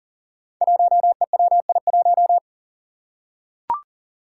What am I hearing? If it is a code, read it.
Morse code, heard as a keyed tone, about 40 wpm, 700 Hz. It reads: 1EWI1
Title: Five Challenging Letter-Number Combinations 40wpm